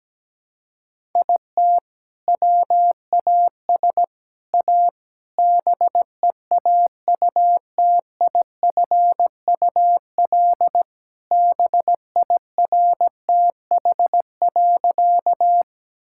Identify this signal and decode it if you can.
Morse code, heard as a keyed tone, about 17 wpm, 700 Hz. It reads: IT WAS A BEAUTIFUL BIRTH.
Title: Sentences from Top 1000 Words - Review Part 5 - 17wpm